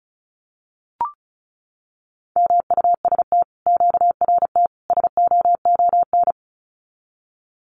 Morse code, heard as a keyed tone, about 36 wpm, 700 Hz. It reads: MUST QRT SOON
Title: Common QSO Phrases Encore 35wpm